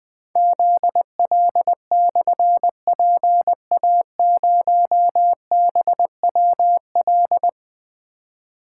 Morse code, heard as a keyed tone, about 20 wpm, 700 Hz. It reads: ZL/PA0BWL
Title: Difficult Call Signs 20wpm